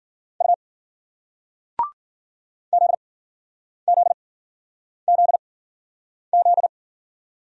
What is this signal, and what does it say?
Morse code, heard as a keyed tone, about 50 wpm, spaced as for 9 wpm, 700 Hz. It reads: U ZZZZ
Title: Speed-Racing Single Letters 40wpm